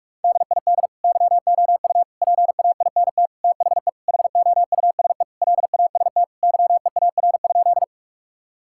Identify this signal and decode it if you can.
Morse code, heard as a keyed tone, about 46 wpm, 700 Hz. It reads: DID YOU PAINT THE HOUSE LAST YEAR?
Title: Sentences from Top 1000 Words - Review Part 6 - 45wpm